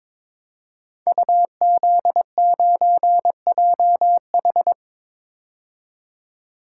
Morse code, heard as a keyed tone, about 22 wpm, 700 Hz. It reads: UZ9JH
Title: Code Groups - 5 Letter:Numbers - 22wpm